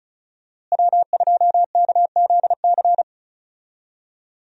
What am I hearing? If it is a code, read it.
Morse code, heard as a keyed tone, about 36 wpm, 700 Hz. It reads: W2KZC
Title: USA Call Signs - 5 Characters - Repeated 3x - 35wpm